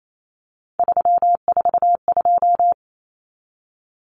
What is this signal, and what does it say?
Morse code, heard as a keyed tone, about 28 wpm, 700 Hz. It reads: 342
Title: Numbers - 3-Digit 28wpm